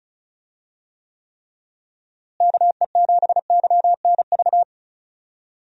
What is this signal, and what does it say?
Morse code, heard as a keyed tone, about 34 wpm, 700 Hz. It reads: KE7YNV
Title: USA Call Signs - 6 Characters - Repeated 3x - 35wpm